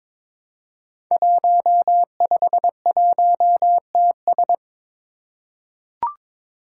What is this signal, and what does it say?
Morse code, heard as a keyed tone, about 22 wpm, 700 Hz. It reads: 151TS
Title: Five Challenging Letter-Number Combinations 22wpm